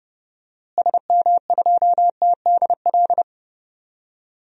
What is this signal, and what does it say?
Morse code, heard as a keyed tone, about 30 wpm, 700 Hz. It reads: SM2TDL
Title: DXCC-100 Call Signs - 6 Characters - Repeated 3x - 30wpm